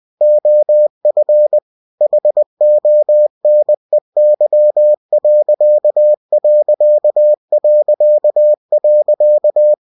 Morse code, heard as a keyed tone, about 20 wpm, 600 Hz. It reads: OF HONEY....
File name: cw-books / winnie-the-pooh / winnie-20wpm-page-0094